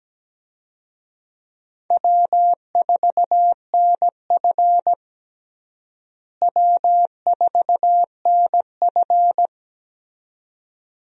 Morse code, heard as a keyed tone, about 17 wpm, 700 Hz. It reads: W4NF W4NF
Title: USA Call Signs - 4 Characters - Repeated 2x - 17wpm